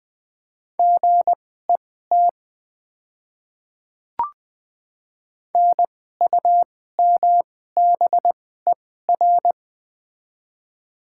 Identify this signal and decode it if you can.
Morse code, heard as a keyed tone, about 20 wpm, spaced as for 14 wpm, 700 Hz. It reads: GET NUMBER
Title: 2x Character Spacing - Top 300 Words - 20wpm